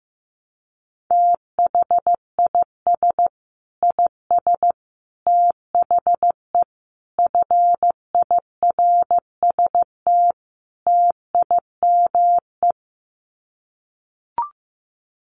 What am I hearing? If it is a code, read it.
Morse code, heard as a keyed tone, about 15 wpm, 700 Hz. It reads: THIS IS THE FIRST TIME
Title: Sets of 5 Words 15wpm